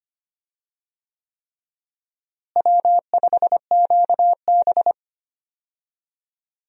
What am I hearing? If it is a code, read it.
Morse code, heard as a keyed tone, about 25 wpm, 700 Hz. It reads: W5QB
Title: USA Call Signs - 4 Characters - Repeated 3x - 25wpm